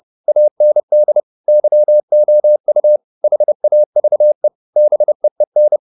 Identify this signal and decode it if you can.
Morse code, heard as a keyed tone, about 30 wpm, 600 Hz. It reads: AND YOU HAVE BEEN
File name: the old man and the sea / the old man and the sea 1 30wpm